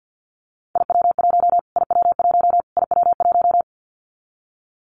Morse code, heard as a keyed tone, about 49 wpm, 700 Hz. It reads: IW1 IW1 IW1
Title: Callsign Prefixes for Top 25 DXCC - Repeated 3x - 50wpm